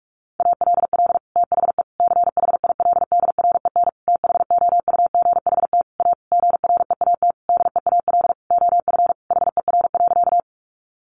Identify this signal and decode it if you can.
Morse code, heard as a keyed tone, about 44 wpm, 700 Hz. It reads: ALL THE CHILDREN THOUGHT A GREAT DEAL OF HER.
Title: Sentences from Top 800 Words 45wpm